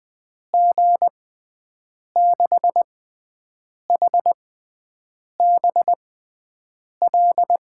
Morse code, heard as a keyed tone, about 20 wpm, spaced as for 7 wpm, 700 Hz. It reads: G6HBL
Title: USA Call Signs - 6 Characters - 6x Character Spacing - 20wpm